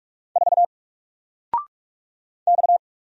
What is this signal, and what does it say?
Morse code, heard as a keyed tone, about 44 wpm, 700 Hz. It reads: 4 X
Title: Single Letter-Number Alt 45wpm